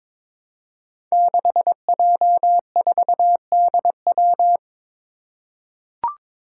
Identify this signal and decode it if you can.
Morse code, heard as a keyed tone, about 22 wpm, 700 Hz. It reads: 6J4DW
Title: Five Challenging Letter-Number Combinations 22wpm